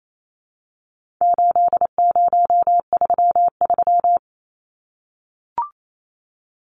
Morse code, heard as a keyed tone, about 28 wpm, 700 Hz. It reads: 8033
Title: Numbers - 4-Digit 28wpm